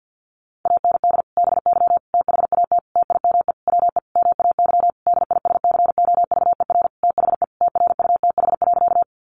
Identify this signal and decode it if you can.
Morse code, heard as a keyed tone, about 50 wpm, 700 Hz. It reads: AND BY THAT TIME WE MAY DISCOVER THE TRUTH.
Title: Sentences from Top 1100 Words 50wpm